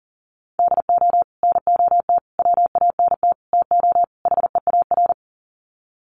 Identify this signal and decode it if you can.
Morse code, heard as a keyed tone, about 38 wpm, 700 Hz. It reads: DO NOT WANT TO HEAR
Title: Sets of 5 Words 40wpm